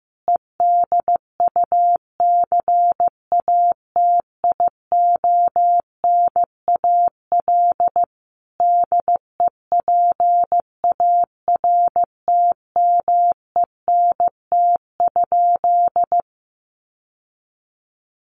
Morse code, heard as a keyed tone, about 15 wpm, 700 Hz. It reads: EDUCATIONAL DEPARTMENT?
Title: Sentences from Top 1200 Words 15wpm